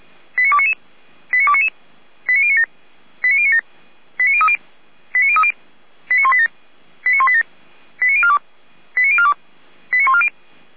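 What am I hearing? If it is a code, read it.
Five-tone selective call (ZVEI), heard as quick runs of five tones, tones of about 70 ms. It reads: 89299 89299 89097 89097 89030 89030 89187 89187 89042 89042 89140
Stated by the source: ZVEI five-tone selective call received on an SDR.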